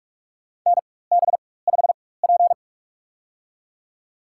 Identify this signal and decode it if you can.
Morse code, heard as a keyed tone, about 42 wpm, spaced as for 21 wpm, 700 Hz. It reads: NB5P
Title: USA Call Signs - 4 Characters - 4x Character Spacing - 45wpm